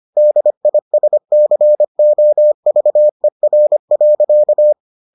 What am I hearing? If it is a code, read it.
Morse code, heard as a keyed tone, about 25 wpm, 600 Hz. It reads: DISCOVER.